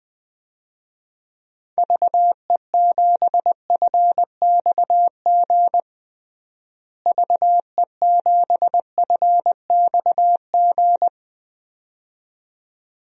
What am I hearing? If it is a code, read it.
Morse code, heard as a keyed tone, about 20 wpm, 700 Hz. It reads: VE7FXG VE7FXG
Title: DXCC-100 Call Signs - 6 Characters - Repeated 2x - 20wpm